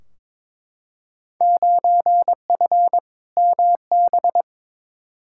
Morse code, heard as a keyed tone, about 22 wpm, 700 Hz. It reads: 9F MB
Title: Field Day Exchanges Encore - 22wpm